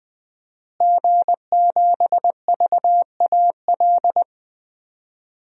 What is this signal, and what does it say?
Morse code, heard as a keyed tone, about 20 wpm, 700 Hz. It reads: G7VAL
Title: DXCC-100 Call Signs - 5 Characters - Repeated 2x - 20wpm